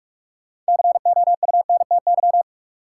Morse code, heard as a keyed tone, about 44 wpm, 700 Hz. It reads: COUNTY
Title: Sentences from Top 1200 Words - New Word Drill Alt - 45wpm